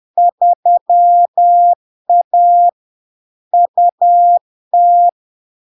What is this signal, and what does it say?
Morse code, heard as a keyed tone, about 10 wpm, 700 Hz. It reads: 3A UT
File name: LARK-FD-2022-Log-in-CW-10wpm-700-Hz